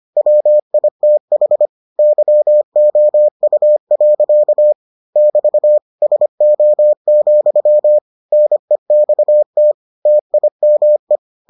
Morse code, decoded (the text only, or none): WITH YOU. <BT> SO, NEXT TIME